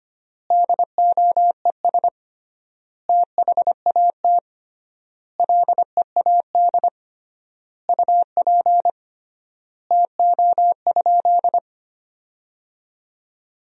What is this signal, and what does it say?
Morse code, heard as a keyed tone, about 25 wpm, 700 Hz. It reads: DOES THAT LEAD UP TO?
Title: 3x Word-Spacing - Sentences from Top 500 Words - 25wpm